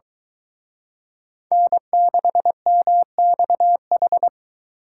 Morse code, heard as a keyed tone, about 23 wpm, 700 Hz. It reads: N6MXH